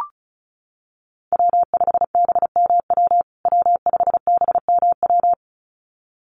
Morse code, heard as a keyed tone, about 34 wpm, 700 Hz. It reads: W5BMW W5BMW
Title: USA Call Signs Repeated 2x 35wpm